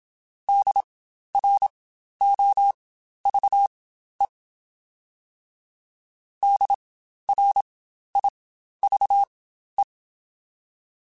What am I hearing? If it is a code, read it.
Morse code, heard as a keyed tone, about 27 wpm, spaced as for 12 wpm, 800 Hz. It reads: DROVE DRIVE
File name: Spaced Alphas / Spaced Alphas mp3s / Ds sp @25